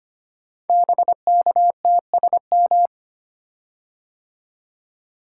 Morse code, heard as a keyed tone, about 25 wpm, 700 Hz. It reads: BKTSM
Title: Code Groups - 5 Letters - 25wpm